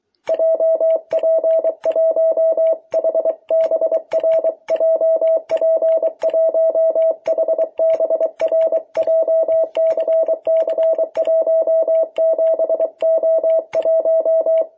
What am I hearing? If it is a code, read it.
Morse code, heard as a keyed tone, about 24 wpm, 660 Hz. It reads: JP1HBR JP1HBR JCC17O1
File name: jh8cbh / 2021cw